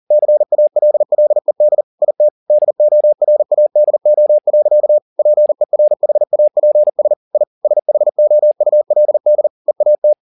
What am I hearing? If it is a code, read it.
Morse code, heard as a keyed tone, about 40 wpm, 600 Hz. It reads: CALLED IT DORADO. PERHAPS I SHOULD EAT
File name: the old man and the sea 2 40wpm